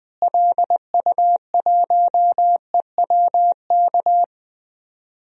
Morse code, heard as a keyed tone, about 20 wpm, 700 Hz. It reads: LU1EWK
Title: DXCC-100 Call Signs - 6 Characters - Repeated 3x - 20wpm